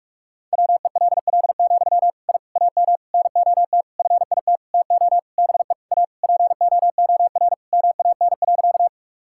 Morse code, heard as a keyed tone, about 45 wpm, 700 Hz. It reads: WELL, I AM NOT FIT TO BE A POOR MAN.
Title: Sentences from Top 1000 Words - Review Part 5 - 45wpm